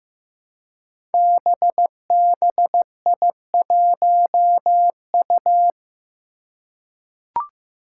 Morse code, heard as a keyed tone, about 15 wpm, 700 Hz. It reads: BBI1U
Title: Five Challenging Letter-Number Combinations 15wpm